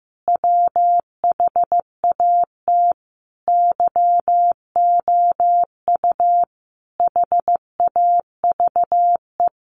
Morse code, heard as a keyed tone, about 15 wpm, 700 Hz. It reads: WHAT YOU HAVE
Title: Sentences from Top 1000 Words 15wpm